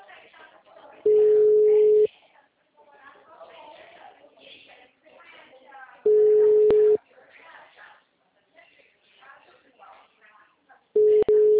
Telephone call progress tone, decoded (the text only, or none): ringback tone